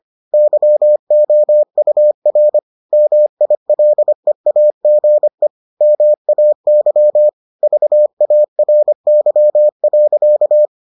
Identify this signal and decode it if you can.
Morse code, heard as a keyed tone, about 25 wpm, 600 Hz. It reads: YOUR MILEAGE MAY VARY.